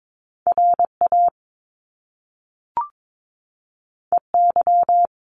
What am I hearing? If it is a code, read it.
Morse code, heard as a keyed tone, about 22 wpm, 700 Hz. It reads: R A E Y
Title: Top Two Letter Combinations in Words 22wpm